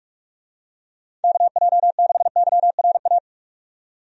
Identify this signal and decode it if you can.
Morse code, heard as a keyed tone, about 47 wpm, 700 Hz. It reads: KJ6YRA
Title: USA Call Signs - 6 Characters - 45wpm